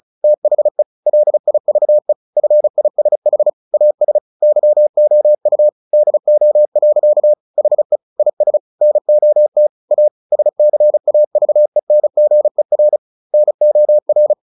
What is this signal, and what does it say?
Morse code, heard as a keyed tone, about 34 wpm, 600 Hz. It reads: THE LIVE FISH AS YOU DO. HE IS NOT A SCAVENGER NOR
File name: the old man and the sea 2 35wpm